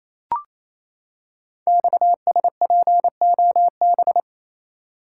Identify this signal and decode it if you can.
Morse code, heard as a keyed tone, about 27 wpm, 700 Hz. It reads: XSPOB
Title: Code Groups - 5 Letters - 28wpm